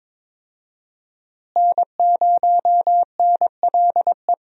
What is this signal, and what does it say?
Morse code, heard as a keyed tone, about 22 wpm, 700 Hz. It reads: N0NLE
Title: USA Call Signs - 5 Characters - Repeated 3x - 22wpm